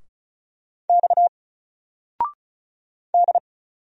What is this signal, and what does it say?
Morse code, heard as a keyed tone, about 36 wpm, 700 Hz. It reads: X D
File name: Single-Letter-Number-35wpm